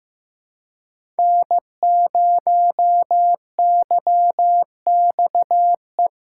Morse code, heard as a keyed tone, about 15 wpm, 700 Hz. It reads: N0YXE
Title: USA Callsign DE Callsign 15wpm